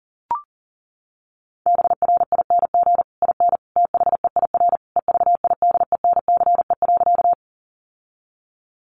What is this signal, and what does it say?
Morse code, heard as a keyed tone, about 40 wpm, 700 Hz. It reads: BRING IN THEIR EVIDENCE.
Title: Sentences from Top 1300 Words 40wpm